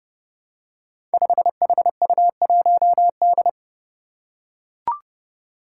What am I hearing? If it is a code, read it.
Morse code, heard as a keyed tone, about 30 wpm, 700 Hz. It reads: 5HU1D
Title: Five Challenging Letter-Number Combinations 30wpm